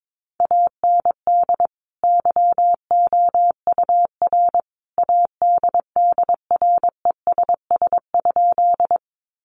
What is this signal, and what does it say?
Morse code, heard as a keyed tone, about 22 wpm, 700 Hz. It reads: AND YOUR ADDRESS?
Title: Sentences from Top 1000 Words 22wpm